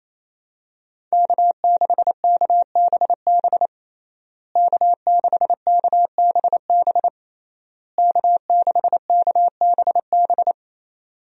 Morse code, heard as a keyed tone, about 28 wpm, 700 Hz. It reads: K6KBB K6KBB K6KBB
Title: USA Call Signs - 5 Characters - Repeated 3x - 28wpm